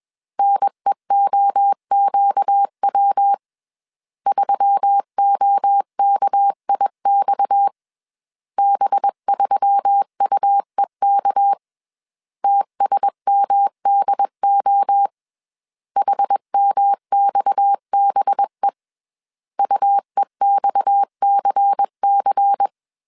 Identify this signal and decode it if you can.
Morse code, heard as a keyed tone, about 21 wpm, 800 Hz. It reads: NEOQW 3OKIX B3UEK TSMDO HMXBE UEXCC
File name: CW / DET / 1 6 1500C S C 20wpm 2116